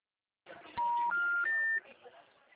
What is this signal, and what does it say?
Special information tone.